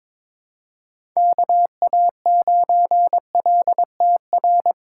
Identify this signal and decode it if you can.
Morse code, heard as a keyed tone, about 21 wpm, 700 Hz. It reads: KA9LTR
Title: USA Call Signs - 6 Characters - 3x Character Spacing - 22wpm